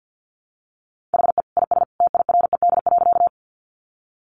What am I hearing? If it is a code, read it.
Morse code, heard as a keyed tone, about 50 wpm, 700 Hz. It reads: HE IS TIRED.
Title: Additional Easy Phrases 50wpm